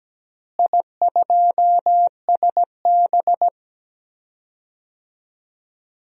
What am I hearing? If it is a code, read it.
Morse code, heard as a keyed tone, about 17 wpm, 700 Hz. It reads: I2SB
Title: DXCC-100 Call Signs - 4 Characters - Repeated 3x - 17wpm